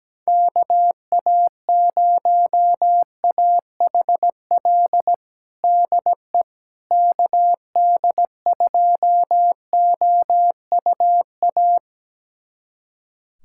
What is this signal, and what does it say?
Morse code, heard as a keyed tone, about 17 wpm, 700 Hz. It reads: KA0AHL DE KD2OUA
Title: USA Callsign DE Callsign 17wpm